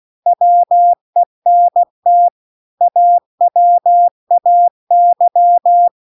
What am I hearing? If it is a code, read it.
Morse code, heard as a keyed tone, about 16 wpm, 700 Hz. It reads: WENT AWAY